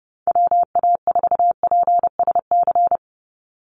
Morse code, heard as a keyed tone, about 30 wpm, 700 Hz. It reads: WA4PSC
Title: Call Signs - 30wpm